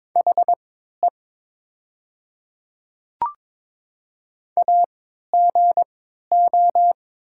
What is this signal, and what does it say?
Morse code, heard as a keyed tone, about 22 wpm, spaced as for 12 wpm, 700 Hz. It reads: HE AGO
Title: 3x Character Spacing - Top 500 Words - 22wpm